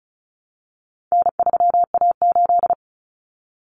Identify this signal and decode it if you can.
Morse code, heard as a keyed tone, about 34 wpm, 700 Hz. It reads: N3A8
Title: Common Cut Numbers - 4-Digit 35wpm